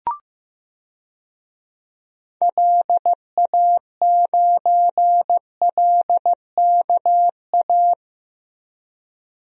Morse code, heard as a keyed tone, about 15 wpm, 700 Hz. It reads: LA9LKA
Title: DXCC-100 Call Signs - 6 Characters - Repeated 2x - 15wpm